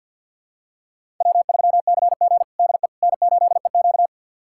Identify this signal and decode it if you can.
Morse code, heard as a keyed tone, about 53 wpm, 700 Hz. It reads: W3CG DE N8EQ
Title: USA Callsign DE Callsign - 4 Characters - 50wpm